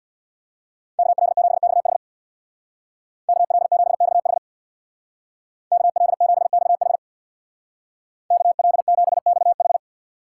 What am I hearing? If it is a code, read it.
Morse code, heard as a keyed tone, about 65 wpm, 700 Hz. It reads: KL7XH KL7XH KL7XH KL7XH